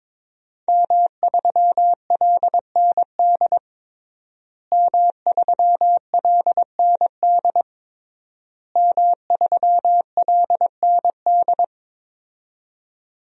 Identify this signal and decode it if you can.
Morse code, heard as a keyed tone, about 21 wpm, 700 Hz. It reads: M3LND M3LND M3LND
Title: DXCC-100 Call Signs - 5 Characters - Repeated 3x - 22wpm